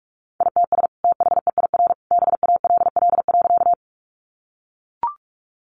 Morse code, heard as a keyed tone, about 44 wpm, 700 Hz. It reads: ITS THEIR BALL.
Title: Sentences from Top 900 Words 45wpm